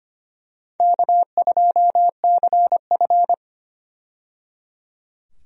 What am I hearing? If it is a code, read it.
Morse code, heard as a keyed tone, about 25 wpm, 700 Hz. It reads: K2CF
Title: USA Call Signs - 4 Characters - 25wpm